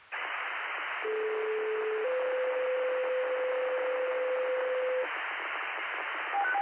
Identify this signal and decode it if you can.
Two-tone page, heard as a one-second tone, then a three-second tone, about 430 then 510 Hz, with noise. It is heard from 1.0 to 5.0 s.